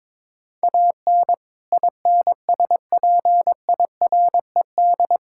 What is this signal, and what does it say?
Morse code, heard as a keyed tone, about 21 wpm, 700 Hz. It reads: AN INSPIRED
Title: Pangram Challenge v2 - 22wpm